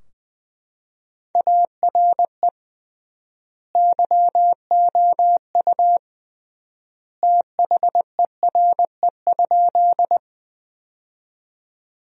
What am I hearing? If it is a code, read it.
Morse code, heard as a keyed tone, about 20 wpm, 700 Hz. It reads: ARE YOU THERE?
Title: 3x Word-Spacing - Sentences from Top 100 Words - 20wpm